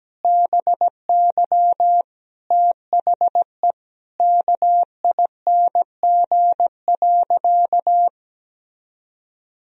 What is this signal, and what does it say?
Morse code, heard as a keyed tone, about 17 wpm, 700 Hz. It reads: BY THE KING.
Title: Sentences from Top 1000 Words - Review Part 4 - 17wpm